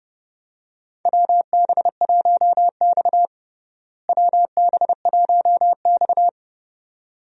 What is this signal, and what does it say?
Morse code, heard as a keyed tone, about 32 wpm, 700 Hz. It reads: WB1X WB1X
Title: USA Call Signs - 4 Characters - Repeated 2x - 30wpm